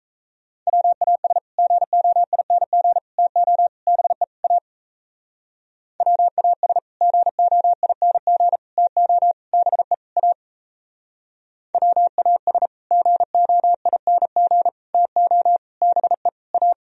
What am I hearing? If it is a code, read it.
Morse code, heard as a keyed tone, about 35 wpm, 700 Hz. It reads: WAS GOING TO BE A WAS GOING TO BE A WAS GOING TO BE A
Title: Speed-Racing - Sets of 5 Words 28wpm